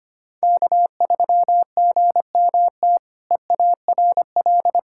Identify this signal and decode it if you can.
Morse code, heard as a keyed tone, about 25 wpm, 700 Hz. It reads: K3GMT EARL
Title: K1USN SST QSO Practice - 25wpm